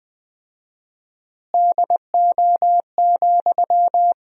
Morse code, heard as a keyed tone, about 20 wpm, 700 Hz. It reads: DO,
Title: 4x Word-Spacing - Sentences from Top 100 Words - 20wpm